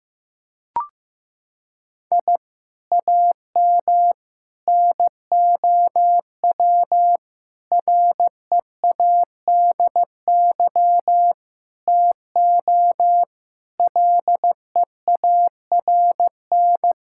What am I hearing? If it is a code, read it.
Morse code, heard as a keyed tone, about 15 wpm, 700 Hz. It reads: I AM NOW READY TO LEARN
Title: Sentences from Top 1000 Words - Review Part 6 - 15wpm